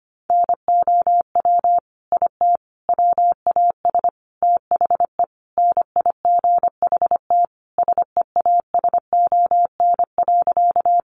Morse code, heard as a keyed tone, about 25 wpm, 700 Hz. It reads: NOW IT WAS THE NIGHT SEASON.
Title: Sentences from Top 1400 Words 25wpm